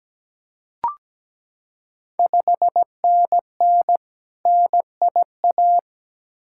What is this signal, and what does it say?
Morse code, heard as a keyed tone, about 17 wpm, 690 Hz. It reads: 5NN NIA
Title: Ontario QSO Party Exchange 17wpm